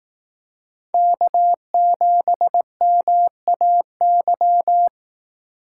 Morse code, heard as a keyed tone, about 18 wpm, 700 Hz. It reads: K7MAY
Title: Speed Racing - USA Call Signs 15wpm